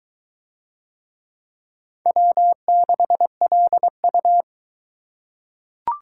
Morse code, heard as a keyed tone, about 20 wpm, 700 Hz. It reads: W6LU